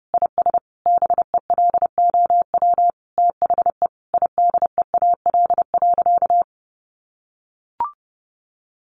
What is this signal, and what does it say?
Morse code, heard as a keyed tone, about 30 wpm, 700 Hz. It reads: IS BELOW THE IDEAL.
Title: Sentences from Top 1400 Words 30wpm